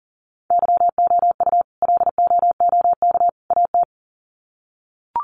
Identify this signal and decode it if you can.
Morse code, heard as a keyed tone, about 40 wpm, 700 Hz. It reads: YOU LOOK AT
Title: Sets of 3 Words 40wpm